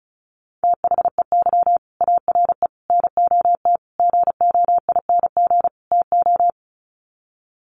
Morse code, heard as a keyed tone, about 34 wpm, 700 Hz. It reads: THEY ARE NOT GOING TO
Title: Sets of 5 Words Encore2 35wpm